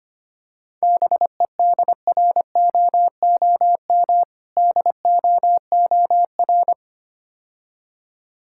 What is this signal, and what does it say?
Morse code, heard as a keyed tone, about 26 wpm, 700 Hz. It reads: BEDROOM DOOR
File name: practice / 25wpm / Difficult Words - Speed-Racing with Hints - 25wpm